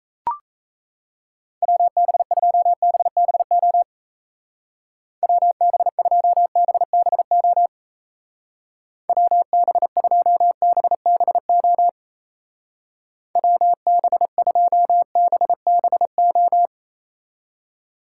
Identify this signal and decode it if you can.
Morse code, heard as a keyed tone, about 35 wpm, 700 Hz. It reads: WB2BBO WB2BBO WB2BBO WB2BBO
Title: Speed Racing - USA 2x3 Call Signs 28wpm